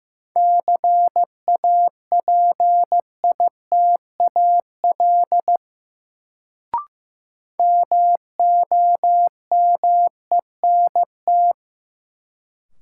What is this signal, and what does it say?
Morse code, heard as a keyed tone, about 15 wpm, 700 Hz. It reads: CAPITAL MOMENT
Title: Sentences from Top 1000 Words - New Word Drill - 15wpm